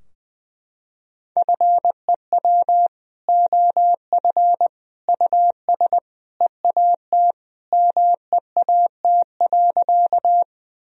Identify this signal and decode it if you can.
Morse code, heard as a keyed tone, about 20 wpm, 700 Hz. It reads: FEW OF US EAT MEAT.
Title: Sentences from Top 1000 Words - Review Part 3 - 20wpm